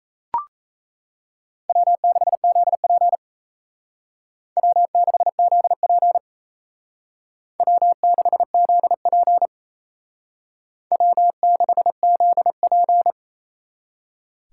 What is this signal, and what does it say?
Morse code, heard as a keyed tone, about 35 wpm, 700 Hz. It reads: W6ZP W6ZP W6ZP W6ZP